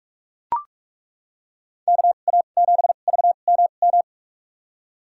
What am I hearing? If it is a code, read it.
Morse code, heard as a keyed tone, about 41 wpm, spaced as for 31 wpm, 700 Hz. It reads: KA7VMM
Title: USA Call Signs - 6 Characters - 2x Character Spacing - 45wpm